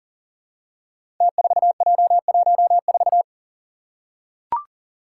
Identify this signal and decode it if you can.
Morse code, heard as a keyed tone, about 40 wpm, 700 Hz. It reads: T4J14
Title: Five Challenging Letter-Number Combinations 40wpm